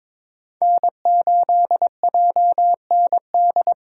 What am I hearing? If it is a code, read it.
Morse code, heard as a keyed tone, about 22 wpm, 700 Hz. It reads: N8JND
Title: USA Call Signs - 5 Characters - 4x Character Spacing - 22wpm